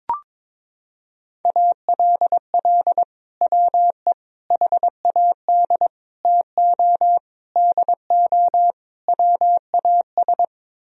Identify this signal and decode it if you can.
Morse code, heard as a keyed tone, about 22 wpm, 700 Hz. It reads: ALL WE HAD TO DO WAS
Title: Sentences from Top 1000 Words - Review Part 4 - 22wpm